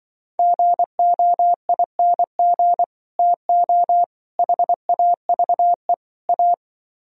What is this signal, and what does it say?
Morse code, heard as a keyed tone, about 23 wpm, 700 Hz. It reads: GOING TO HAVE A
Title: Speed-Racing - Sets of 4 Words 20wpm